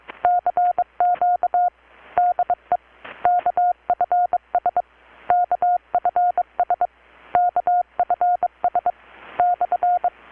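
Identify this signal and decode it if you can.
Morse code, heard as a keyed tone, about 22 wpm, 720 Hz. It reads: CQ DE KFS KFS KFS /
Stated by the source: Morse from KPH's Night of Nights XX maritime radiotelegraphy event, received in Chiapas, Mexico about 3,800 km away on a Kenwood R-600 with a grounded 10 m long-wire antenna.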